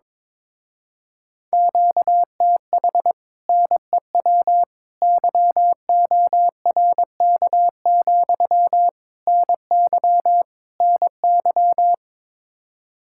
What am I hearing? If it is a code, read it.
Morse code, heard as a keyed tone, about 22 wpm, 700 Hz. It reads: QTH NEW YORK, NY NY
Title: Speed-Racing Common QSO Phrases - 22wpm